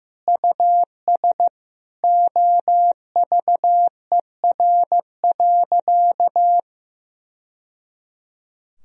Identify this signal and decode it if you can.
Morse code, heard as a keyed tone, about 15 wpm, 700 Hz. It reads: US OVER.
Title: Sentences from Top 1000 Words - Review Part 1 - 15wpm